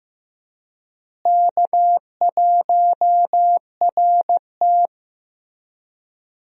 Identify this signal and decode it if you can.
Morse code, heard as a keyed tone, about 15 wpm, 700 Hz. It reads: K1RT
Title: USA Call Signs - 4 Characters - Repeated 3x - 15wpm